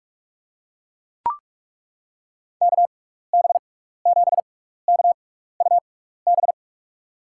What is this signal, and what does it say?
Morse code, heard as a keyed tone, about 42 wpm, spaced as for 15 wpm, 700 Hz. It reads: KB7KUB
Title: USA Call Signs - 6x Character Spacing - 45wpm